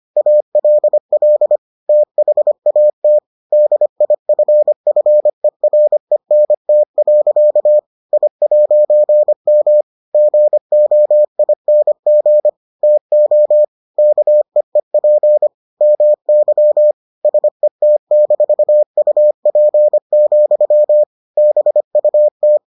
Morse code, heard as a keyed tone, about 25 wpm, 600 Hz. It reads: ALL THAT DIFFERENT. I'M GOING TO KEEP MY SET-UP, BUT